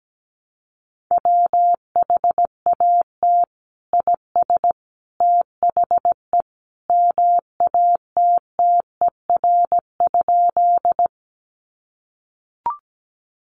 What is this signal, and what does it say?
Morse code, heard as a keyed tone, about 17 wpm, 700 Hz. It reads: WHAT IS THE MATTER?
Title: Sentences from Top 700 Words 17wpm